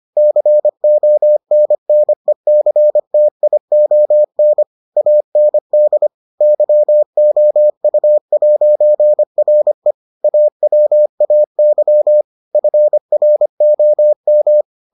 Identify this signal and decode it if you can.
Morse code, heard as a keyed tone, about 25 wpm, 600 Hz. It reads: CONNECTION AND YOU'RE AWAY FROM